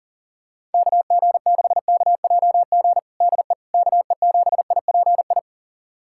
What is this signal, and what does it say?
Morse code, heard as a keyed tone, about 40 wpm, 690 Hz. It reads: KG6KJG DE KE7IPI
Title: USA Callsign DE Callsign - 6 Characters - 40wpm